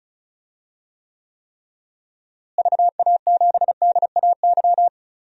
Morse code, heard as a keyed tone, about 36 wpm, 700 Hz. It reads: VA7DAY
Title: DXCC-100 Call Signs - 6 Characters - Repeated 3x - 35wpm